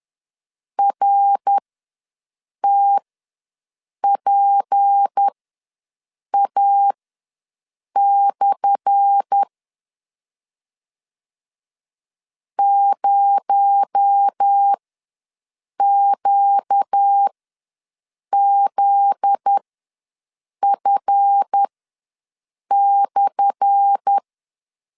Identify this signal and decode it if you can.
Morse code, heard as a keyed tone, about 10 wpm, spaced as for 6 wpm, 800 Hz. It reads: RTPA/ 0QZF/
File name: LEZIONI / 10 SIGNS / 10A SLASH ws 10wpm 4121